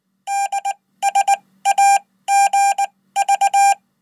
Morse code, heard as a keyed tone, about 19 wpm, spaced as for 15 wpm, 800 Hz. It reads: DSAGV
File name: Code